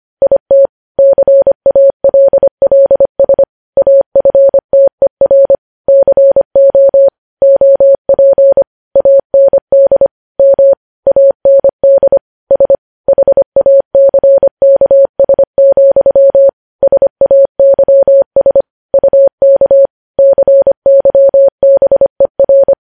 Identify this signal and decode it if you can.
Morse code, heard as a keyed tone, about 25 wpm, 550 Hz. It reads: IT CALLS AFTER CO OP AND M AND S HACKS, SAYS UK CYBER
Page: news-headlines-in-morse-code-at-25-wpm